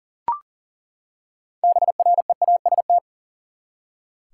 Morse code, heard as a keyed tone, about 40 wpm, 700 Hz. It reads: BREAST
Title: Six Letter Words - 40wpm